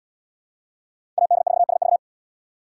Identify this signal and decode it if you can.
Morse code, heard as a keyed tone, about 69 wpm, 700 Hz. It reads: AD4SV